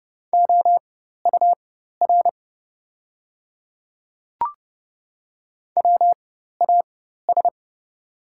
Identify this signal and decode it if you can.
Morse code, heard as a keyed tone, about 30 wpm, spaced as for 14 wpm, 700 Hz. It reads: OUR WAS
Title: 4x Character Spacing - Top 200 Words - 30wpm